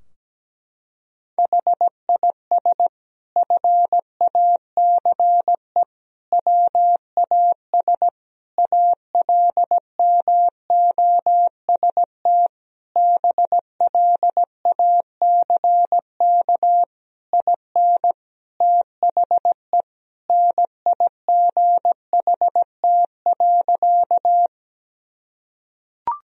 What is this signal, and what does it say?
Morse code, heard as a keyed tone, about 17 wpm, 700 Hz. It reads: HIS FACE WAS ALMOST BLACK IN THE NIGHT.
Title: Sentences from Top 1000 Words - Review Part 4 - 17wpm